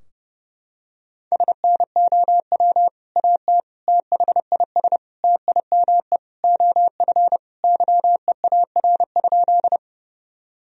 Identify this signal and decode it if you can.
Morse code, heard as a keyed tone, about 31 wpm, 700 Hz. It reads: SNOW AT THIS TIME OF YEAR?
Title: Sentences from Top 1000 Words - Review Part 3 - 30wpm